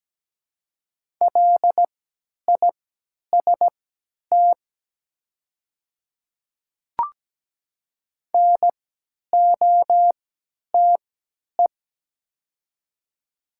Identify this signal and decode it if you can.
Morse code, heard as a keyed tone, about 17 wpm, spaced as for 10 wpm, 700 Hz. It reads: LIST NOTE
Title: 3x Character Spacing - Top 500 Words - 17wpm